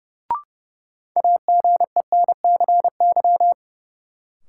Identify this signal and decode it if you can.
Morse code, heard as a keyed tone, about 32 wpm, 700 Hz. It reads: AGENCY
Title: Sentences from Top 1300 Words - New Word Drill - 30wpm